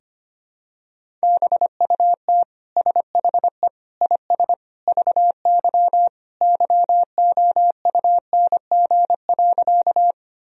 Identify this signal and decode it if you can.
Morse code, heard as a keyed tone, about 25 wpm, 700 Hz. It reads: BUT SHE IS VY YOUNG.
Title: Sentences from Top 400 Words and Abbreviations 25wpm